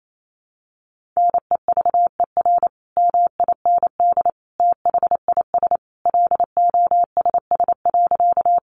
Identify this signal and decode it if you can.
Morse code, heard as a keyed tone, about 28 wpm, 700 Hz. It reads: NEVER MIND THIS LOSS.
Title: Sentences from Top 1400 Words 28wpm